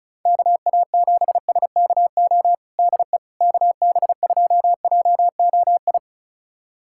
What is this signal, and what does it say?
Morse code, heard as a keyed tone, about 35 wpm, 700 Hz. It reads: KA7SKO DE KB2JOI
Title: USA Callsign DE Callsign - 6 Characters - 35wpm